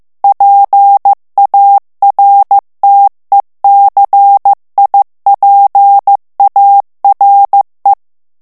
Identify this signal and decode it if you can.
Morse code, heard as a keyed tone, about 15 wpm, 800 Hz. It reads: PARTECIPARE
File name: morse mp3 / ITALIEN texte clair 2 vitesse 14 mots minute